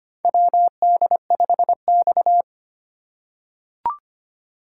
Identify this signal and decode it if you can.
Morse code, heard as a keyed tone, about 25 wpm, 700 Hz. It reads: WD5X
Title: USA Call Signs - 4 Characters - 3x Character Spacing - 25wpm